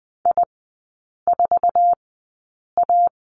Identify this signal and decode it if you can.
Morse code, heard as a keyed tone, about 20 wpm, spaced as for 8 wpm, 700 Hz. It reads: I4A
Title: 067 Lesson 12e - Letter d - Three Characters 20wpm